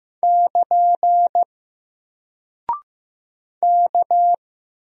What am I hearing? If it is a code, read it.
Morse code, heard as a keyed tone, about 15 wpm, 700 Hz. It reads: <KN> K
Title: Single Character (Letter:Number:Punctuation:Prosign) 15wpm